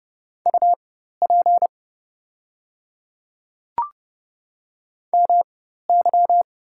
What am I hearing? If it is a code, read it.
Morse code, heard as a keyed tone, about 30 wpm, spaced as for 14 wpm, 700 Hz. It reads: UP MY
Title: 4x Character Spacing - Top 300 Words - 30wpm